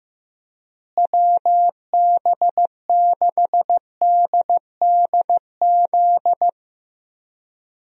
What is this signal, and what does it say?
Morse code, heard as a keyed tone, about 15 wpm, 700 Hz. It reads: WB6DDZ
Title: USA Call Signs - 6 Characters - Repeated 3x - 15wpm